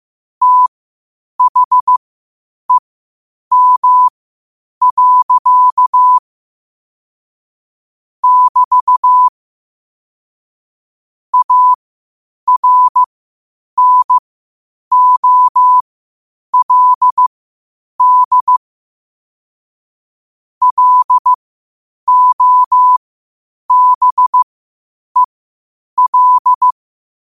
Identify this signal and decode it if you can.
Morse code, heard as a keyed tone, about 15 wpm, spaced as for 8 wpm, 1000 Hz. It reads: THEM. <BT> ARNOLD LOBEL
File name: qotdfiles / 8wpm